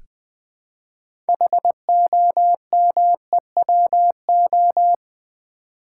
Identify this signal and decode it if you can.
Morse code, heard as a keyed tone, about 20 wpm, 690 Hz. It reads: HOMEWO
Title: Build A Word - 3 Letters At A Time - 8-Letter Words - 20wpm